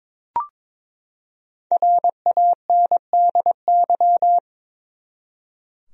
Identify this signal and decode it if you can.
Morse code, heard as a keyed tone, about 22 wpm, 700 Hz. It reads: RANDY
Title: USA Common Names - Max 5 Letters - 22wpm